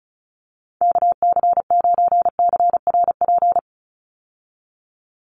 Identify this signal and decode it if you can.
Morse code, heard as a keyed tone, about 34 wpm, 700 Hz. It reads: KC9CRP
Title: USA Call Signs 35wpm